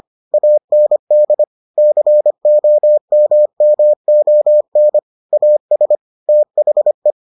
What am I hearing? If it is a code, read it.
Morse code, heard as a keyed tone, about 25 wpm, 600 Hz. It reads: AND COMMON AS THE